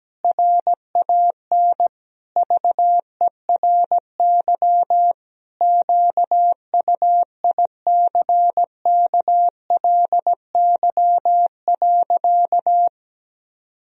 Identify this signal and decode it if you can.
Morse code, heard as a keyed tone, about 17 wpm, 700 Hz. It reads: RAN VERY QUICKLY.
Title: Sentences from Top 1000 Words - Review Part 3 - 17wpm